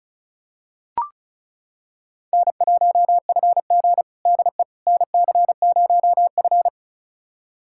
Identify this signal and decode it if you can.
Morse code, heard as a keyed tone, about 34 wpm, 700 Hz. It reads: N1FG DE NC0F
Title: USA Callsign DE Callsign - 4 Characters - 35wpm